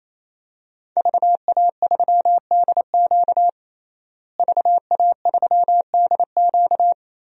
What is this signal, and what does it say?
Morse code, heard as a keyed tone, about 27 wpm, 700 Hz. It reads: VA3DQ VA3DQ
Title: DXCC-100 Call Signs - 5 Characters - Repeated 3x - 28wpm